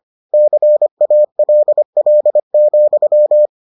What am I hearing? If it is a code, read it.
Morse code, heard as a keyed tone, about 25 wpm, 600 Hz. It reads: CALL,